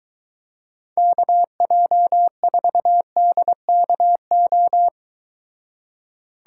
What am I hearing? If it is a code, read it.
Morse code, heard as a keyed tone, about 23 wpm, 700 Hz. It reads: KJ4DKO